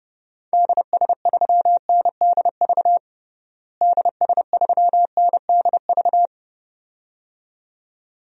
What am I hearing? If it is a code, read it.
Morse code, heard as a keyed tone, about 30 wpm, 700 Hz. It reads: DS3NDV DS3NDV
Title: DXCC-100 Call Signs - 6 Characters - Repeated 2x - 30wpm